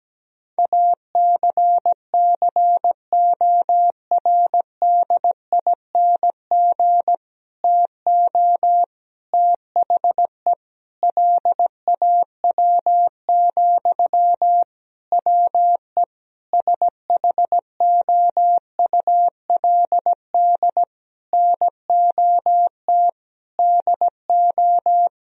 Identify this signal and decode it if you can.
Morse code, heard as a keyed tone, about 17 wpm, 700 Hz. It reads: ACCORDING TO THE LAW, WE SHOULD NOT DO
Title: Sentences from Top 1000 Words - Review Part 4 - 17wpm